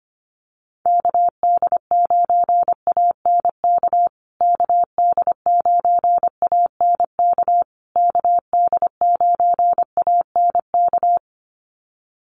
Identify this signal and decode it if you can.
Morse code, heard as a keyed tone, about 25 wpm, 700 Hz. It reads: KD9ANK KD9ANK KD9ANK
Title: USA Call Signs Repeated 3x 25wpm